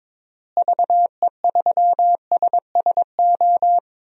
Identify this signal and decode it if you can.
Morse code, heard as a keyed tone, about 22 wpm, 700 Hz. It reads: VE3SSO
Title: DXCC-100 Call Signs - 6 Characters - Repeated 3x - 22wpm